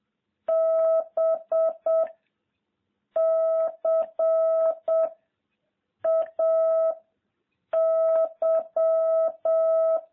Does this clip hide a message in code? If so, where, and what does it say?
Yes: Morse code from 0.5 to 10.0 s, heard as a keyed tone, about 7 wpm, 660 Hz. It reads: B C A Y